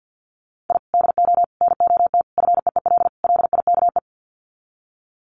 Morse code, heard as a keyed tone, about 50 wpm, 700 Hz. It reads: I DO NOT FEEL LIKE
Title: Sets of 5 Words Encore 50wpm